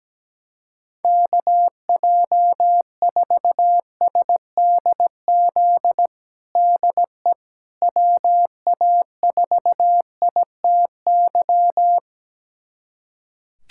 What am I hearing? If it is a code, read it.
Morse code, heard as a keyed tone, about 17 wpm, 700 Hz. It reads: KJ4SDZ DE WA4ITY
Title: USA Callsign DE Callsign - 6 Characters - 17wpm